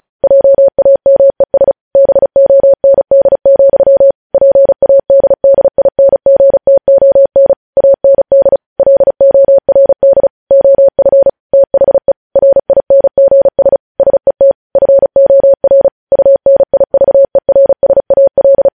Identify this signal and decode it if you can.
Morse code, heard as a keyed tone, about 36 wpm, 550 Hz. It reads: JAMES BOND, PADDINGTON AND LORD OF THE RINGS SET FOR UNIVERSAL
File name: podcast / News headlines-1x-35x35